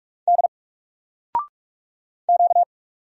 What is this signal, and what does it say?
Morse code, heard as a keyed tone, about 44 wpm, 700 Hz. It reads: D Q
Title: Single Letter-Number Alt 45wpm